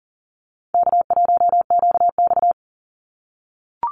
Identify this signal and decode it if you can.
Morse code, heard as a keyed tone, about 40 wpm, 700 Hz. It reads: K1QX
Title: Easy Call Signs - 40wpm